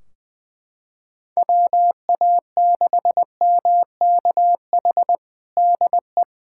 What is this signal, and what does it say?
Morse code, heard as a keyed tone, about 20 wpm, 700 Hz. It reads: WA6MKH DE
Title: Practice QSO - Exchange 4 - Ending - 20wpm